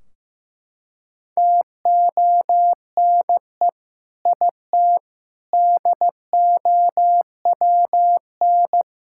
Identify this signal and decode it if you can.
Morse code, heard as a keyed tone, about 16 wpm, 700 Hz. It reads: TONE IT DOWN
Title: Sentences from Top 1600 Words 15wpm